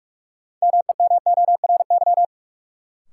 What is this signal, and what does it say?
Morse code, heard as a keyed tone, about 46 wpm, 700 Hz. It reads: MEMORY
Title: Sentences from Top 1300 Words - New Word Drill Alt - 45wpm